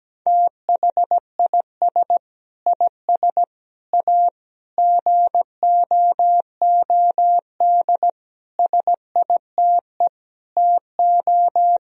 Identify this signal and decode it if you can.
Morse code, heard as a keyed tone, about 17 wpm, 700 Hz. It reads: THIS IS A GOOD SITE TO
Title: Sentences from Top 1000 Words - Review Part 4 - 17wpm